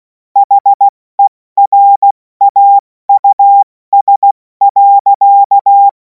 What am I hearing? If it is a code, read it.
Morse code, heard as a keyed tone, about 16 wpm, 800 Hz. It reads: HERAUS.
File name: Morsekurs / Witze / Witz 03. März 2026 70 F80